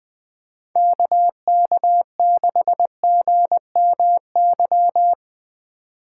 Morse code, heard as a keyed tone, about 19 wpm, 700 Hz. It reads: KK6GMY
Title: Speed Racing - USA 2x3 Call Signs 17wpm